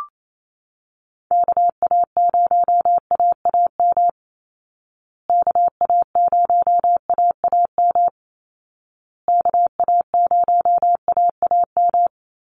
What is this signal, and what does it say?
Morse code, heard as a keyed tone, about 28 wpm, 700 Hz. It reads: KA0AAM KA0AAM KA0AAM
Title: Call Signs Repeated 2x - 28wpm